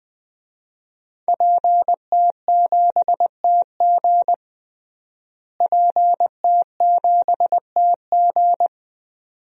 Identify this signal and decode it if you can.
Morse code, heard as a keyed tone, about 21 wpm, 700 Hz. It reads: PT7TG PT7TG
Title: DXCC-100 Call Signs - 5 Characters - Repeated 2x - 20wpm